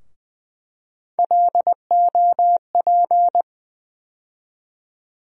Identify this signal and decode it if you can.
Morse code, heard as a keyed tone, about 20 wpm, 700 Hz. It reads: LOP
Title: USA Call Sign Postfixes - 3 Letters 20wpm